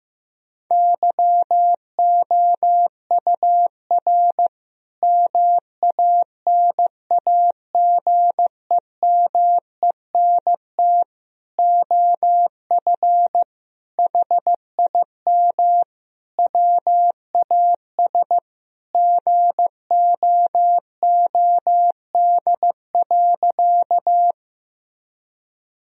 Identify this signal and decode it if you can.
Morse code, heard as a keyed tone, about 15 wpm, 700 Hz. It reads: YOUR MANAGEMENT OF HIM WAS GOOD.
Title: Sentences from Top 1000 Words - Review Part 4 - 15wpm